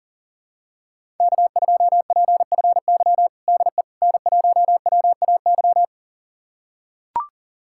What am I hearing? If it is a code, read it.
Morse code, heard as a keyed tone, about 37 wpm, 700 Hz. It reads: K2PFY DE N1WAY
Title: USA Callsign DE Callsign 40wpm